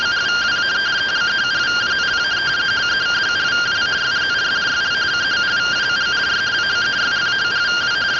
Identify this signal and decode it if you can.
Two-tone teleprinter signal, about 170 Hz shift.